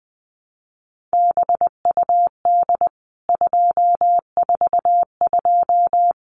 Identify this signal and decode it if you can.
Morse code, heard as a keyed tone, about 20 wpm, 700 Hz. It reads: BUD 242
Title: CWT Exchange 20wpm